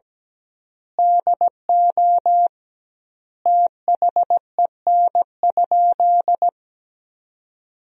Morse code, heard as a keyed tone, about 17 wpm, 700 Hz. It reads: DO THEN?
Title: 2x Word-Spacing - Sentences from Top 100 Words - 17wpm